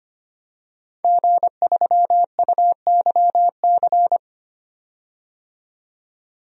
Morse code, heard as a keyed tone, about 25 wpm, 700 Hz. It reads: G3UYC
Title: DXCC-100 Call Signs - 5 Characters - 25wpm